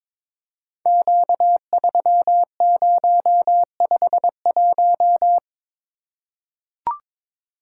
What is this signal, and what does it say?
Morse code, heard as a keyed tone, about 23 wpm, 700 Hz. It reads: Q3051
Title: Code Groups - 5 Letter:Numbers - 22wpm